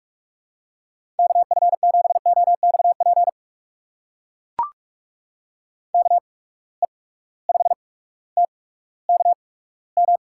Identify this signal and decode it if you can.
Morse code, heard as a keyed tone, about 43 wpm, spaced as for 19 wpm, 700 Hz. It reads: KF7OXP KE5TKM
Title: USA Call Signs - 6 Characters - 8x Character Spacing - 45wpm